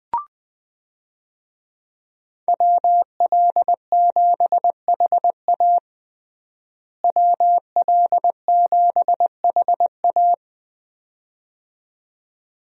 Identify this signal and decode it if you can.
Morse code, heard as a keyed tone, about 20 wpm, 700 Hz. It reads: WL7HA WL7HA
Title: USA Call Signs - 5 Characters - Repeated 2x - 20wpm